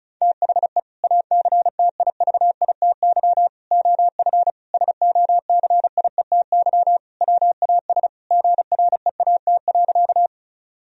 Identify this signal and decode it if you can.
Morse code, heard as a keyed tone, about 36 wpm, 700 Hz. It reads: THE ACTIVITY OF SOCIETY WAS GREAT.
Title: Sentences from Top 1000 Words - Review Part 4 - 35wpm